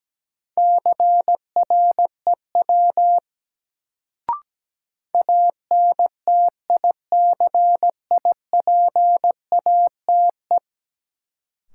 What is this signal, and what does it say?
Morse code, heard as a keyed tone, about 17 wpm, 700 Hz. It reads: CREW ANTICIPATE
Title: Sentences from Top 1500 Words - New Word Drill - 17wpm